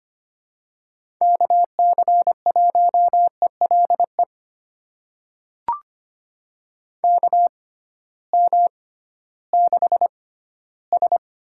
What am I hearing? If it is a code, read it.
Morse code, heard as a keyed tone, about 25 wpm, spaced as for 14 wpm, 700 Hz. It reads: KC1ELE KM6S
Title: USA Call Signs - 6 Characters - 6x Character Spacing - 25wpm